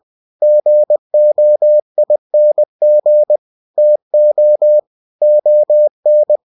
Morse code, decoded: GOING TO ON

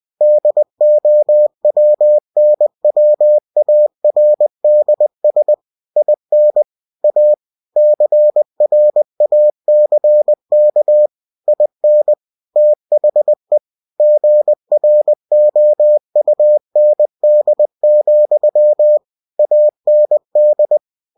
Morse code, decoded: DOWNWARDS IN A CRACK IN THE GROUND, AND

